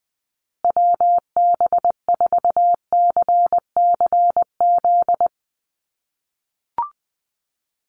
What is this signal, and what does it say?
Morse code, heard as a keyed tone, about 20 wpm, 700 Hz. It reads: WB4CCZ
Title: USA Call Signs Repeated 2x 20wpm